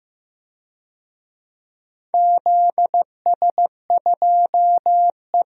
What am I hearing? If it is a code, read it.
Morse code, heard as a keyed tone, about 15 wpm, 700 Hz. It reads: ZS2E
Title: DXCC-100 Call Signs - 4 Characters - 15wpm